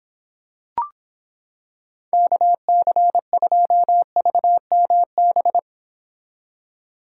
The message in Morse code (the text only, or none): KC2VMB